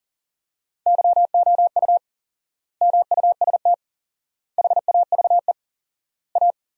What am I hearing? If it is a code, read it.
Morse code, heard as a keyed tone, about 40 wpm, 700 Hz. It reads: YOU MUST HAVE A
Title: 4x Word-Spacing - Sentences from Top 500 Words - 40wpm